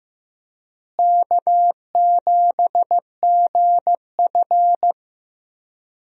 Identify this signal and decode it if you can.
Morse code, heard as a keyed tone, about 15 wpm, 700 Hz. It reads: K7GF
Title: USA Call Signs - 4 Characters - Repeated 3x - 15wpm